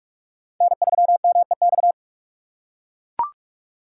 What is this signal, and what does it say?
Morse code, heard as a keyed tone, about 44 wpm, 700 Hz. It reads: N3MEX
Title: USA Call Signs - 3x Character Spacing - 45wpm